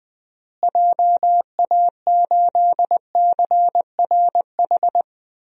Morse code, heard as a keyed tone, about 20 wpm, 700 Hz. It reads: JA8CRH
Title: DXCC-100 Call Signs - 6 Characters - Repeated 3x - 20wpm